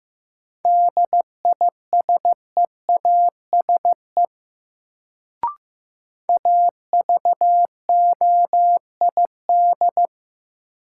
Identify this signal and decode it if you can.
Morse code, heard as a keyed tone, about 15 wpm, 700 Hz. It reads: DISEASE AVOID
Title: Sentences from Top 1000 Words - Review Part 6 - New Word Drill - 15wpm